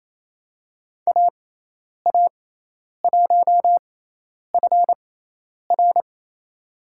Morse code, heard as a keyed tone, about 28 wpm, spaced as for 10 wpm, 700 Hz. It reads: AA1FR
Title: USA Call Signs - 5 Characters - 6x Character Spacing - 28wpm